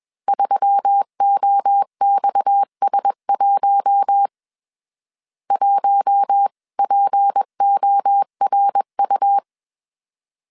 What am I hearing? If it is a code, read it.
Morse code, heard as a keyed tone, about 21 wpm, 800 Hz. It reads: 3OXS1 1PORU
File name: CW / DET / 1 6 1500C S C 20wpm 2116